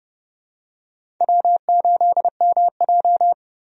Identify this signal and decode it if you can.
Morse code, heard as a keyed tone, about 29 wpm, 700 Hz. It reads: W8MJ
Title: USA Call Signs - 4 Characters - 30wpm